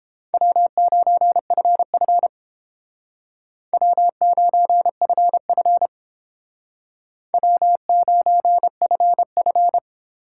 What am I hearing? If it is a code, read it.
Morse code, heard as a keyed tone, about 30 wpm, 700 Hz. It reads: W9FF W9FF W9FF